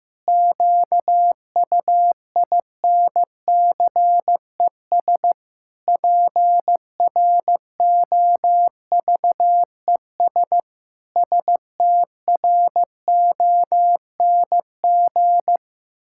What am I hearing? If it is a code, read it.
Morse code, heard as a keyed tone, about 16 wpm, 700 Hz. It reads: QUINCES PROVES STRONG
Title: Pangram Challenge v2 - 15wpm